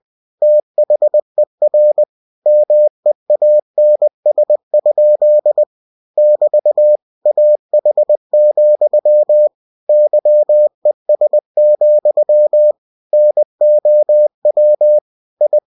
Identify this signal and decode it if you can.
Morse code, heard as a keyed tone, about 20 wpm, 600 Hz. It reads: THER MEANS? <BT> AH, YES, NOW I